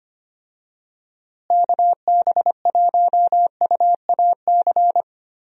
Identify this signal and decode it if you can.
Morse code, heard as a keyed tone, about 25 wpm, 700 Hz. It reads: KB1UAC
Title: USA Call Signs - 6 Characters - Repeated 3x - 25wpm